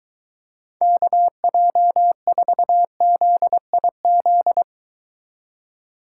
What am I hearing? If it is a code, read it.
Morse code, heard as a keyed tone, about 23 wpm, 700 Hz. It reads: KJ4ZIZ